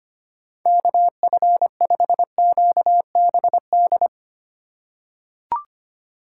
Morse code, heard as a keyed tone, about 25 wpm, 700 Hz. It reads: KF5QBD
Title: USA Call Signs - 6 Characters - 2x Character Spacing - 25wpm